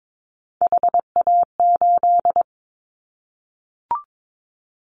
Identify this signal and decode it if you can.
Morse code, heard as a keyed tone, about 22 wpm, 700 Hz. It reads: HA8
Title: Callsign Prefixes for Top 25 DXCC - Repeated 3x - 22wpm